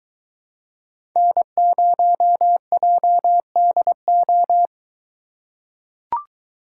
Morse code, heard as a keyed tone, about 22 wpm, 700 Hz. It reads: N0JDO